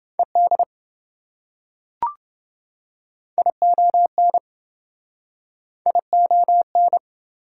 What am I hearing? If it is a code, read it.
Morse code, heard as a keyed tone, about 27 wpm, 700 Hz. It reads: ED ION ION